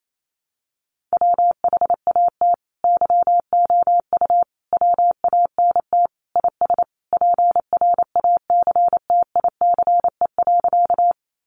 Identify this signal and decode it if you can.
Morse code, heard as a keyed tone, about 28 wpm, 700 Hz. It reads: WHAT YOU WANT IS PRACTICE.
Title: Sentences from Top 700 Words 28wpm